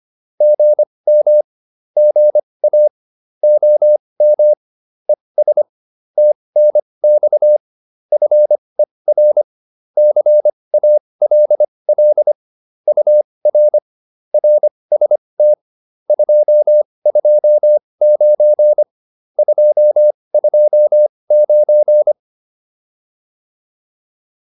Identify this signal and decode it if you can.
Morse code, heard as a keyed tone, about 25 wpm, spaced as for 20 wpm, 600 Hz. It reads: GM GA OM ES TNX FER CALL UR RST 229 229
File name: QSO407 20